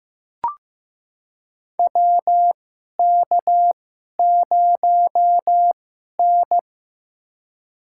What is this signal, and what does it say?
Morse code, heard as a keyed tone, about 15 wpm, spaced as for 11 wpm, 700 Hz. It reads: WK0N
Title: USA Call Signs - 4 Characters - 2x Character Spacing - 15wpm